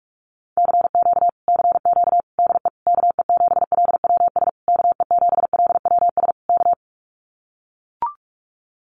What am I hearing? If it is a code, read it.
Morse code, heard as a keyed tone, about 44 wpm, 700 Hz. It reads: CQ CQ DE KE7LWS KE7LWS K
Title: Common QSO Phrases Encore 45wpm